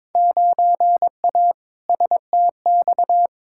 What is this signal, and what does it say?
Morse code, heard as a keyed tone, about 22 wpm, 700 Hz. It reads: 9A STX
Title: Field Day Exchanges - 22wpm